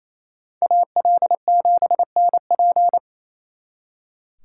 Morse code, heard as a keyed tone, about 36 wpm, 700 Hz. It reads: AL7NP